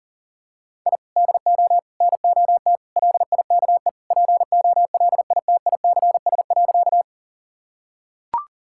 Morse code, heard as a keyed tone, about 37 wpm, 700 Hz. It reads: I DO NOT LIKE POLITICS.
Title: Sentences from Top 1000 Words - Review Part 6 - 40wpm